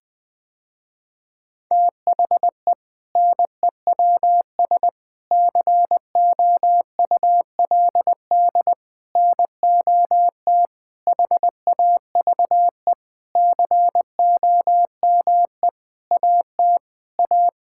Morse code, heard as a keyed tone, about 20 wpm, 700 Hz. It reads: THE NEWS COULD NOT HAVE COME AT A
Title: Sentences with 8-Word Phrases - Spoken Beginning - 20wpm